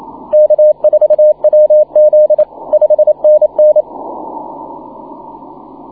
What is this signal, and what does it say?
Morse code, heard as a keyed tone, about 27 wpm, 610 Hz, with radio noise. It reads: K4WZ 5NN
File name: wp / vp5 w5sl 12m cw 15 mar 2010 1644z